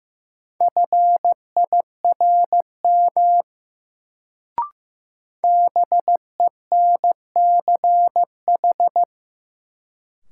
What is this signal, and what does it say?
Morse code, heard as a keyed tone, about 16 wpm, 700 Hz. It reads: FIRM BENCH
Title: Sentences from Top 1400 Words - New Word Drill - 15wpm